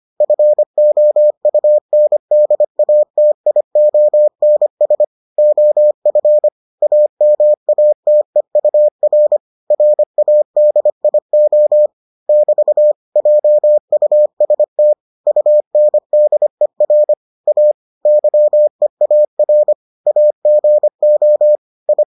Morse code, heard as a keyed tone, about 25 wpm, 600 Hz. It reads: FOUNDATIONS OF AMATEUR RADIO <BT> JUST UNDER A YEAR AGO I